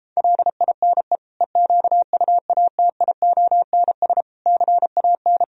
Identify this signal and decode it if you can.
Morse code, heard as a keyed tone, about 33 wpm, 700 Hz. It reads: LINE EQUATIONS CAN